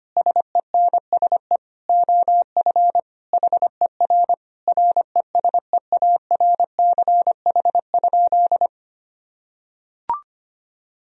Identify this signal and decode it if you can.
Morse code, heard as a keyed tone, about 25 wpm, 700 Hz. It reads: SENSE OF HER RESEARCH?
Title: Sentences from Top 1000 Words - Review Part 4 - 25wpm